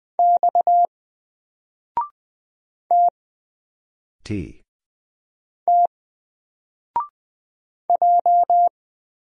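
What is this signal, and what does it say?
Morse code, heard as a keyed tone, about 20 wpm, 700 Hz. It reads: X T T J
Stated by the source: Single letter and number Morse drill at 20 wpm, from N4LCD.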